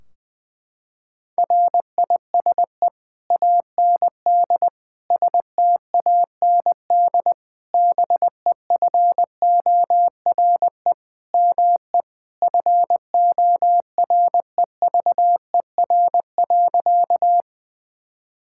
Morse code, heard as a keyed tone, about 20 wpm, 700 Hz. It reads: RISE AND STAND BEFORE ME FOREVER.
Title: Sentences from Top 1600 Words 20wpm